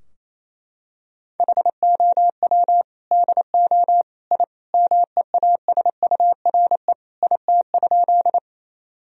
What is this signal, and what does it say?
Morse code, heard as a keyed tone, about 28 wpm, 700 Hz. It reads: HOW DO I MEASURE IT?
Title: Sentences from Top 1000 Words - Review Part 2 - 28wpm